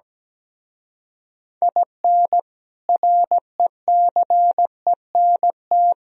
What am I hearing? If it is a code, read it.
Morse code, heard as a keyed tone, about 17 wpm, 700 Hz. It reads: IN RECENT
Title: Speed-Racing - Sets of 3 Words 17wpm